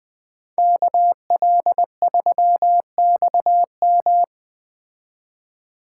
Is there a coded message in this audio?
Yes: Morse code from 0.6 to 4.2 s, heard as a keyed tone, about 19 wpm, 700 Hz. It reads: KL3XM